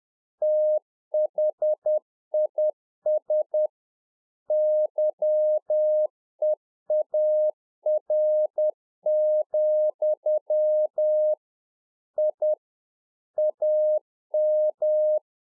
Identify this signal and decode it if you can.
Morse code, heard as a keyed tone, about 10 wpm, 610 Hz. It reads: THIS YEAR, I AM
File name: 2011-04-mp3 / RICK JJ0MPI 10WPM